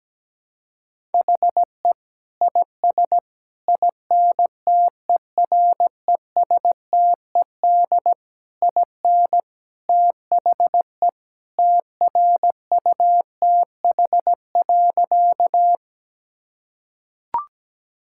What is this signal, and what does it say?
Morse code, heard as a keyed tone, about 17 wpm, 700 Hz. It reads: HE IS INTERESTED IN THE TRUTH.
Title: Sentences from Top 1000 Words - Review Part 6 - 17wpm